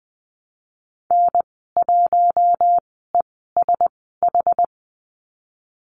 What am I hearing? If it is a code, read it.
Morse code, heard as a keyed tone, about 20 wpm, spaced as for 15 wpm, 700 Hz. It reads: N1ESH
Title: 082 Lesson 14g - Number 2 - Call Signs 20wpm